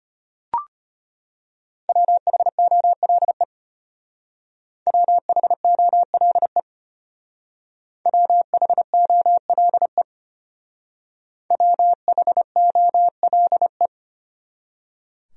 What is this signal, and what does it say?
Morse code, heard as a keyed tone, about 32 wpm, 700 Hz. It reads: WHOLE WHOLE WHOLE WHOLE